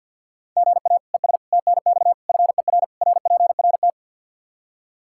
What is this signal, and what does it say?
Morse code, heard as a keyed tone, about 51 wpm, 700 Hz. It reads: GA ES TNX FER RPRT
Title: Practice QSO - Exchange 1 - Beginning - 50wpm